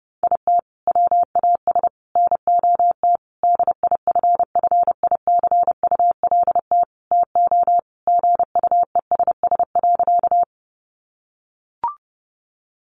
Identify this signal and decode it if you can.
Morse code, heard as a keyed tone, about 28 wpm, 700 Hz. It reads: IT WAS NOT DIFFICULT TO GUESS.
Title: Sentences from Top 1500 Words 30wpm